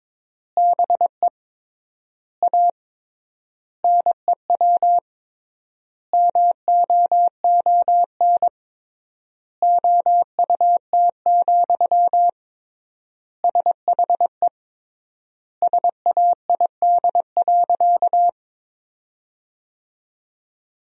Morse code, heard as a keyed tone, about 22 wpm, 700 Hz. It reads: BE A NEW MOON OUT, SHE SAID.
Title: 3x Word-Spacing - Sentences from Top 500 Words - 22wpm